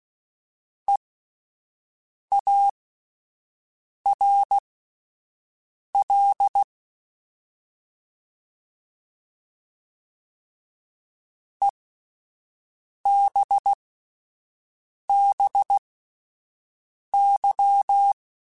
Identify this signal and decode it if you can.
Morse code, heard as a keyed tone, about 16 wpm, spaced as for 5 wpm, 800 Hz. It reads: EARL EBBY
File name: Alphabetical / Alphabetical mp3s / Es @5